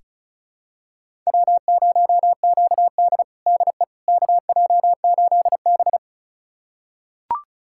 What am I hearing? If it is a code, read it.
Morse code, heard as a keyed tone, about 36 wpm, 700 Hz. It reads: W0QD DE KJ8B
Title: USA Callsign DE Callsign - 4 Characters - 35wpm